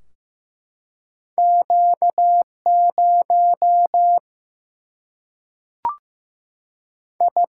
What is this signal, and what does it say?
Morse code, heard as a keyed tone, about 16 wpm, 700 Hz. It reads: Q 0 I
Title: Single-Letter Single-Number 15wpm